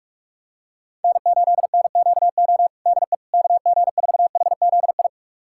Morse code, heard as a keyed tone, about 46 wpm, 700 Hz. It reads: N8NQO DE KG4HZI
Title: USA Callsign DE Callsign 45wpm